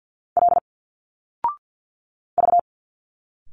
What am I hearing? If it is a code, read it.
Morse code, heard as a keyed tone, about 51 wpm, 700 Hz. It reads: L V
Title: Single Letters 50wpm